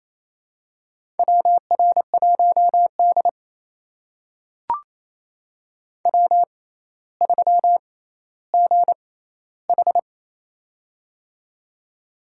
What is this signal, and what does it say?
Morse code, heard as a keyed tone, about 28 wpm, spaced as for 14 wpm, 700 Hz. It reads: WR1D W3GH
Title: USA Call Signs - 4 Characters - 6x Character Spacing - 28wpm